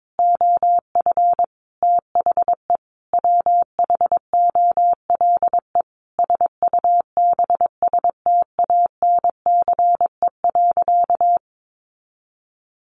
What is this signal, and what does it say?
Morse code, heard as a keyed tone, about 22 wpm, 700 Hz. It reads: OF THE WHOLE SUBSTANCE.
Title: Sentences from Top 1500 Words 22wpm